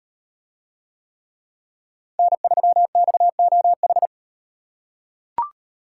Morse code, heard as a keyed tone, about 37 wpm, 700 Hz. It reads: N3XOH